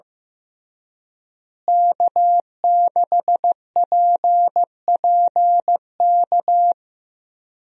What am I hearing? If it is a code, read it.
Morse code, heard as a keyed tone, about 15 wpm, 700 Hz. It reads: K6PPK
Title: USA Call Signs - 5 Characters - Repeated 3x - 15wpm